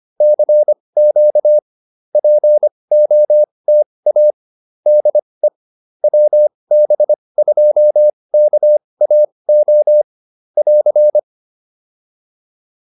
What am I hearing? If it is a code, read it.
Morse code, heard as a keyed tone, about 25 wpm, spaced as for 20 wpm, 590 Hz. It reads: CQ POTA DE WB2KAO <AR>
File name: POTA208 20